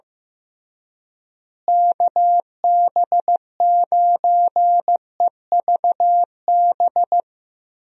Morse code, heard as a keyed tone, about 16 wpm, 700 Hz. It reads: KB9EVB
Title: USA Call Signs - 6 Characters - Repeated 3x - 15wpm